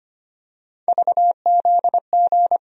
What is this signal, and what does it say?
Morse code, heard as a keyed tone, about 25 wpm, 700 Hz. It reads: VZG
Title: USA Call Sign Postfixes - 3 Letters 25wpm